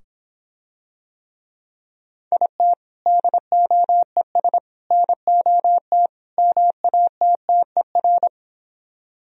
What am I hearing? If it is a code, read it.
Morse code, heard as a keyed tone, about 26 wpm, 700 Hz. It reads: IT DOES NOT MATTER